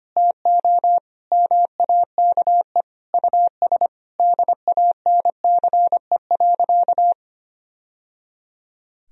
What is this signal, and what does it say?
Morse code, heard as a keyed tone, about 25 wpm, 700 Hz. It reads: TO MAKE US DANCE.
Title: Sentences from Top 1000 Words - Review Part 4 - 25wpm